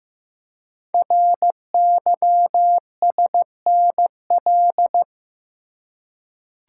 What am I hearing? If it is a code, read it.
Morse code, heard as a keyed tone, about 15 wpm, 700 Hz. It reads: RYSNL
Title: Code Groups - 5 Letters - 15wpm